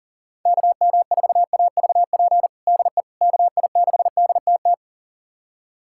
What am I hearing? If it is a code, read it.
Morse code, heard as a keyed tone, about 37 wpm, 700 Hz. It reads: KM4AVP DE KI6DTT
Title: USA Callsign DE Callsign - 6 Characters - 40wpm